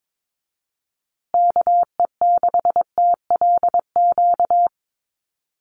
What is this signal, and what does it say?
Morse code, heard as a keyed tone, about 22 wpm, 700 Hz. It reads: KE6TLQ
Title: USA Call Signs Repeated 2x 22wpm